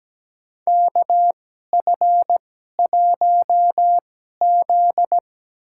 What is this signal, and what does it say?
Morse code, heard as a keyed tone, about 17 wpm, spaced as for 12 wpm, 700 Hz. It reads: KF1Z
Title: USA Call Signs - 4 Characters - 2x Character Spacing - 17wpm